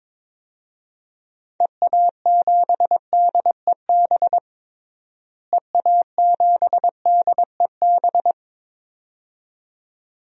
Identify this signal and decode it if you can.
Morse code, heard as a keyed tone, about 21 wpm, 700 Hz. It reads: EA7DEB EA7DEB
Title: DXCC-100 Call Signs - 6 Characters - Repeated 2x - 22wpm